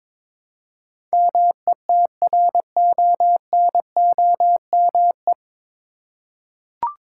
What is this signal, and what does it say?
Morse code, heard as a keyed tone, about 22 wpm, 700 Hz. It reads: METRONOME
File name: Difficult Words with Hints - 22wpm